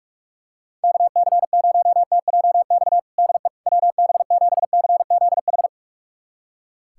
Morse code, heard as a keyed tone, about 41 wpm, 700 Hz. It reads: KC0TJX DE WB7CZH
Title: USA Callsign DE Callsign 45wpm